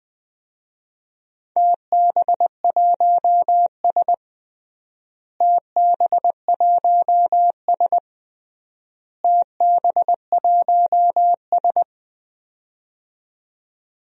Morse code, heard as a keyed tone, about 20 wpm, 700 Hz. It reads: TB1S TB1S TB1S
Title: DXCC-100 Call Signs - 4 Characters - Repeated 3x - 20wpm